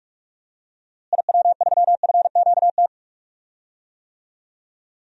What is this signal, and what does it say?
Morse code, heard as a keyed tone, about 44 wpm, 700 Hz. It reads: IW3FQT
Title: DXCC-100 Call Signs - 6 Characters - Repeated 3x - 45wpm